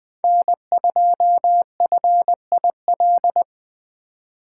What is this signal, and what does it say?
Morse code, heard as a keyed tone, about 20 wpm, 700 Hz. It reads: N2FIL
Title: Speed Racing - USA 1x3 Call Signs 17wpm